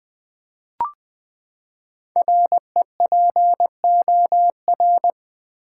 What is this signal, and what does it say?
Morse code, heard as a keyed tone, about 20 wpm, 700 Hz. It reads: REPOR
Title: Build A Word - 5 Letters At A Time - 10-Letter Words - 20wpm